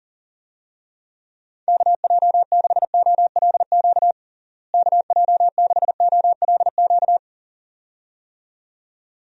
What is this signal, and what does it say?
Morse code, heard as a keyed tone, about 40 wpm, 700 Hz. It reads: KJ6OLQ KJ6OLQ
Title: USA Call Signs - 6 Characters - Repeated 2x - 40wpm